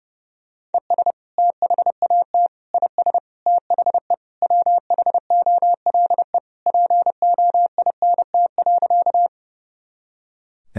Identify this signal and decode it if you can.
Morse code, heard as a keyed tone, about 30 wpm, 700 Hz. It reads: ES THAT IS THE WHOLE POINT.
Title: Sentences from Top 400 Words and Abbreviations 30wpm